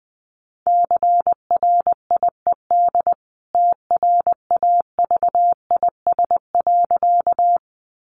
Morse code, heard as a keyed tone, about 20 wpm, 700 Hz. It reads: CRIED TRAVIS.
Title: A Sound Of Thunder 20wpm